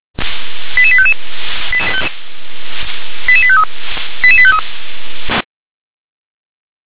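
Five-tone selective call (ZVEI), heard as quick runs of five tones, tones of about 70 ms. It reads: 88855 88853 88853
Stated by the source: Five-tone ZVEI selective call received on SDR.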